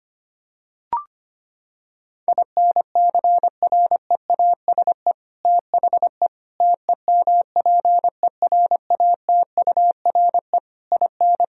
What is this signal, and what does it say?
Morse code, heard as a keyed tone, about 25 wpm, 700 Hz. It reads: INCREASE THE TEMPERATURE IN
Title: Sentences from Top 1000 Words - Review Part 4 - 25wpm